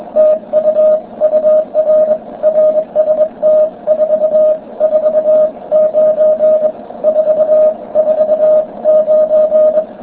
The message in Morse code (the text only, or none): TU UR RST 449 449